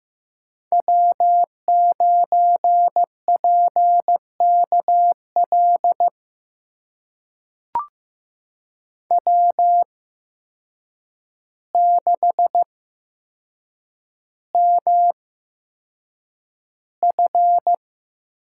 Morse code, heard as a keyed tone, about 15 wpm, spaced as for 7 wpm, 700 Hz. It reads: W9PKL W6MF
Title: USA Call Signs - 8x Character Spacing - 15wpm